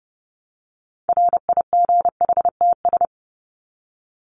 Morse code, heard as a keyed tone, about 30 wpm, 700 Hz. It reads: RIGHTS
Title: ES:S Suffix - Top 100 Words with Base Words 30wpm